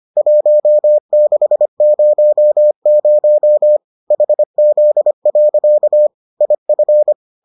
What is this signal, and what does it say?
Morse code, heard as a keyed tone, about 25 wpm, 600 Hz. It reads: 1600 HZ. IF